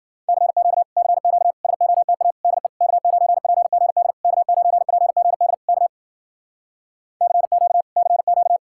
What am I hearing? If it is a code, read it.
Morse code, heard as a keyed tone, about 52 wpm, 700 Hz. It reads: CQ CQ SOTA DE K9PGD K9PGD K CQ CQ